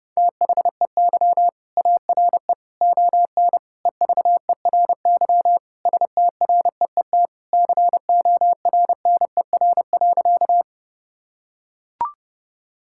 Morse code, heard as a keyed tone, about 32 wpm, 700 Hz. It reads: THEY ARE ON EVERY STREET CORNER.
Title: Sentences from Top 1600 Words 30wpm